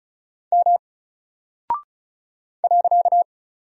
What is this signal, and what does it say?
Morse code, heard as a keyed tone, about 36 wpm, 700 Hz. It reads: M .
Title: Single Character (Letter:Number:Punctuation:Prosign) 35wpm